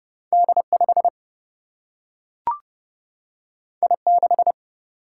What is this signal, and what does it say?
Morse code, heard as a keyed tone, about 30 wpm, 700 Hz. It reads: D5 I6
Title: Single-Letter Single-Number 30wpm